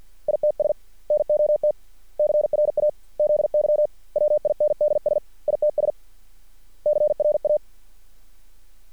Morse code, heard as a keyed tone, about 50 wpm, 600 Hz. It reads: ITS NOT CRAZY WINDS ITS CRA
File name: archived / CRSnet